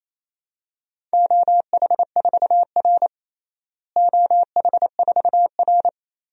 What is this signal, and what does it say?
Morse code, heard as a keyed tone, about 28 wpm, 700 Hz. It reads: OH4R OH4R
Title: DXCC-100 Call Signs - 4 Characters - Repeated 3x - 28wpm